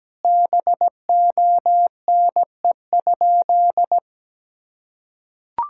Morse code, heard as a keyed tone, about 17 wpm, 700 Hz. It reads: BONE?
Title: Sentences from Top 1600 Words 17wpm